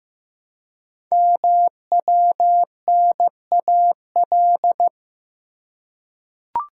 Morse code, heard as a keyed tone, about 15 wpm, 700 Hz. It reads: MWNAL
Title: Code Groups - 5 Letters - 15wpm